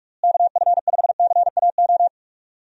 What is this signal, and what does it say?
Morse code, heard as a keyed tone, about 41 wpm, 700 Hz. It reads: KF5CAO
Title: USA Call Signs - 6 Characters - Repeated 3x - 45wpm